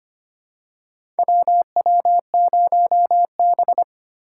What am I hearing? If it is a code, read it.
Morse code, heard as a keyed tone, about 25 wpm, 700 Hz. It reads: WW0B
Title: USA Call Signs - 4 Characters - 4x Character Spacing - 25wpm